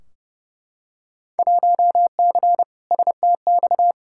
Morse code, heard as a keyed tone, about 30 wpm, 700 Hz. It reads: 1C STX
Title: Field Day Exchanges Encore - 30wpm